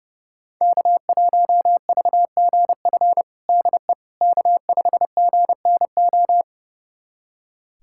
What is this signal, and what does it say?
Morse code, heard as a keyed tone, about 30 wpm, 700 Hz. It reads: K1VGF DE K5GNO
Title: USA Callsign DE Callsign - 5 Characters - 30wpm